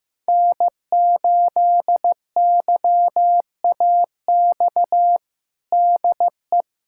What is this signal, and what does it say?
Morse code, heard as a keyed tone, about 15 wpm, 700 Hz. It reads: N8YAX DE
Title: USA Callsign DE Callsign - 5 Characters - 15wpm